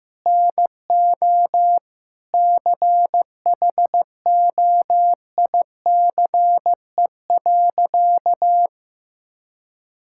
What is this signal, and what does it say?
Morse code, heard as a keyed tone, about 15 wpm, 700 Hz. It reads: NO CHOICE.
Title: Sentences from Top 1000 Words - Review Part 6 - 15wpm